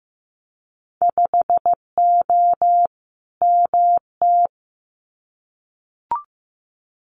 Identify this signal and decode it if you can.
Morse code, heard as a keyed tone, about 15 wpm, 700 Hz. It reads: 5O MT
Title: Winter Field Day Exchanges Encore 15wpm